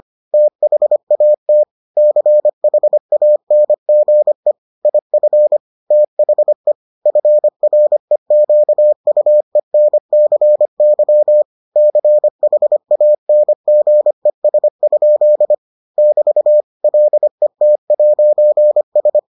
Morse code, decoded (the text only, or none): THAT CHANGE IF THE FREQUENCY CHANGES? <BT> LET'S